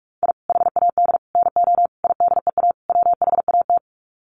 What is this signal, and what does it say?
Morse code, heard as a keyed tone, about 44 wpm, 700 Hz. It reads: I HAD NO IDEA WHAT
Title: Sets of 5 Words Encore 45wpm